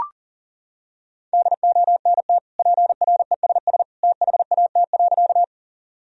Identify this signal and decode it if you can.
Morse code, heard as a keyed tone, about 40 wpm, 700 Hz. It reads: DONT PRESS THAT.
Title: Sentences from Top 1000 Words - Review Part 1 - 40wpm